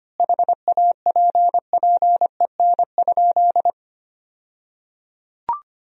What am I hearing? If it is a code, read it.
Morse code, heard as a keyed tone, about 25 wpm, 700 Hz. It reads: HAPPEN?
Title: 2x Word-Spacing - Sentences from Top 400 Words - 25wpm